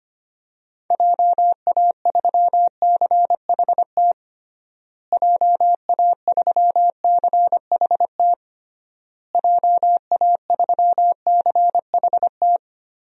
Morse code, heard as a keyed tone, about 25 wpm, 700 Hz. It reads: JA3CHT JA3CHT JA3CHT
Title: DXCC-100 Call Signs - 6 Characters - Repeated 3x - 25wpm